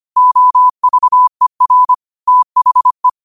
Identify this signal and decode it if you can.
Morse code, heard as a keyed tone, about 25 wpm, 1000 Hz. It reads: OVER THE
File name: qotdfiles / 25wpm